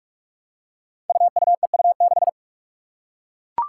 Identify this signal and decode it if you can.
Morse code, heard as a keyed tone, about 43 wpm, 700 Hz. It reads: UUEU6
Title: Five Challenging Letter-Number Combinations 45wpm